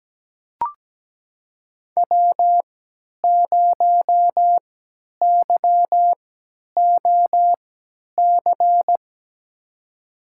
Morse code, heard as a keyed tone, about 17 wpm, spaced as for 10 wpm, 700 Hz. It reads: W0YOC
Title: USA Call Signs - 5 Characters - 3x Character Spacing - 17wpm